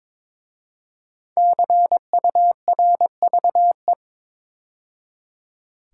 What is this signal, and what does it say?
Morse code, heard as a keyed tone, about 22 wpm, 700 Hz. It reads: CURVE
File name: 5 Letter Words - First 3 Letters Given - 22wpm